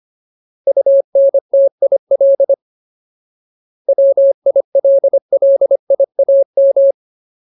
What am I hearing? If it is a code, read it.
Morse code, heard as a keyed tone, about 25 wpm, 550 Hz. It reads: UNTIL WILLIAM
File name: Ultimate Word List / 25WPM Ultimate